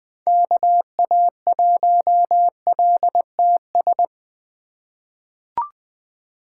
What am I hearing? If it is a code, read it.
Morse code, heard as a keyed tone, about 20 wpm, 700 Hz. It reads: KA1LTS
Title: USA Call Signs - 6 Characters - 3x Character Spacing - 20wpm